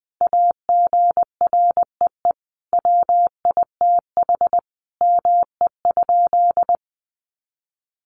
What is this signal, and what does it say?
Morse code, heard as a keyed tone, about 20 wpm, 700 Hz. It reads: AGREE WITH ME?
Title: Sentences from Top 1100 Words 20wpm